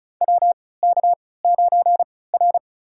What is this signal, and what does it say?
Morse code, heard as a keyed tone, about 34 wpm, spaced as for 20 wpm, 700 Hz. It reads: WK9R
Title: USA Call Signs - 4 Characters - 3x Character Spacing - 35wpm